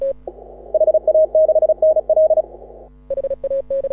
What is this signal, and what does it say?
Morse code, heard as a keyed tone, about 35 wpm, 630 Hz. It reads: HA6NL
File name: AUDIO2018 / HA6OI CW